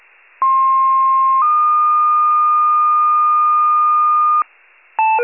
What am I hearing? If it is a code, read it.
Two-tone page, heard as a one-second tone, then a three-second tone, about 1060 then 1220 Hz, with noise. It is heard from 0.4 to 4.4 s.
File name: wcalert